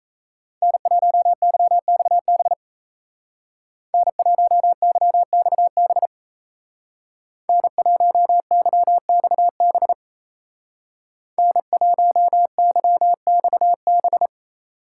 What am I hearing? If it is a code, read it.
Morse code, heard as a keyed tone, about 36 wpm, 700 Hz. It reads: N1YXB N1YXB N1YXB N1YXB